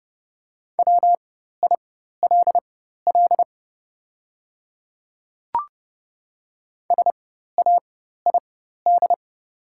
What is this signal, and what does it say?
Morse code, heard as a keyed tone, about 30 wpm, spaced as for 14 wpm, 700 Hz. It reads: WILL SAID
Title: 4x Character Spacing - Top 100 Words - 30wpm